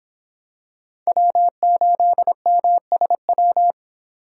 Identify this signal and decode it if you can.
Morse code, heard as a keyed tone, about 26 wpm, 700 Hz. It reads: W8MSW